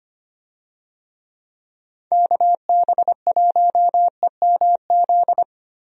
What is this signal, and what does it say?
Morse code, heard as a keyed tone, about 25 wpm, 700 Hz. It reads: KB1EMZ
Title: USA Call Signs - 6 Characters - 25wpm